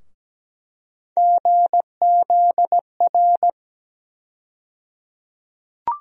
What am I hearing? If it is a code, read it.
Morse code, heard as a keyed tone, about 17 wpm, 700 Hz. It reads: GZR
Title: USA Call Sign Postfixes - 3 Letters 17wpm